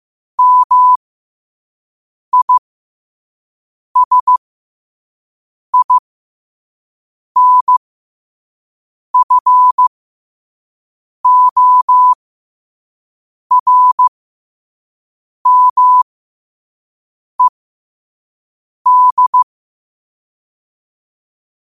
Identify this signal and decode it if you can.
Morse code, heard as a keyed tone, about 15 wpm, spaced as for 5 wpm, 1000 Hz. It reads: MISINFORMED
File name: qotdfiles / 5wpm